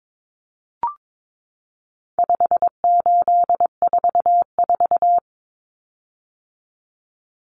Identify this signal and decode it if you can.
Morse code, heard as a keyed tone, about 22 wpm, 700 Hz. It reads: 5844
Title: Common Cut Numbers - 4-Digit 22wpm